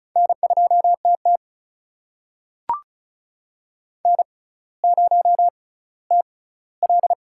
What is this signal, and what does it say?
Morse code, heard as a keyed tone, about 36 wpm, spaced as for 18 wpm, 700 Hz. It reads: N2TT N0TL
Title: USA Call Signs - 4 Characters - 6x Character Spacing - 35wpm